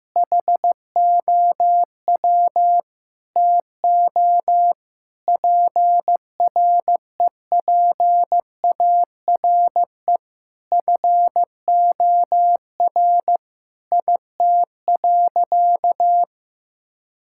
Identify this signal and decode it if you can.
Morse code, heard as a keyed tone, about 15 wpm, 700 Hz. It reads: HOW TO PREPARE FOR IT.
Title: Sentences from Top 1000 Words - Review Part 6 - 15wpm